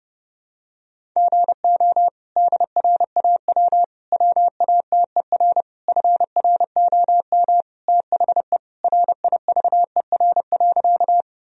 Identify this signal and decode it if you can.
Morse code, heard as a keyed tone, about 30 wpm, 700 Hz. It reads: GO DRAW WATER FROM THE RIVER.
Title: Sentences from Top 1000 Words - Review Part 1 - 30wpm